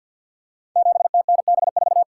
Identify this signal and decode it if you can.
Morse code, heard as a keyed tone, about 49 wpm, 700 Hz. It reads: 7TNB4
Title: Code Groups - 5 Letter:Numbers - 50wpm